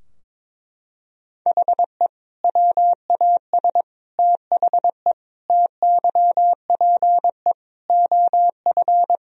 Morse code, heard as a keyed tone, about 22 wpm, 700 Hz. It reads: HE WAS THE TYPE OF
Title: Sentences from Top 1000 Words - Review Part 3 - 22wpm